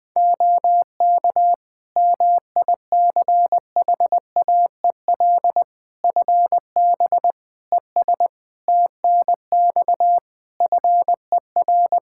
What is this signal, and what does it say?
Morse code, heard as a keyed tone, about 20 wpm, 700 Hz. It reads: OK MICHAEL FB ES TNX FER
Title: K1USN SST QSO Practice 20wpm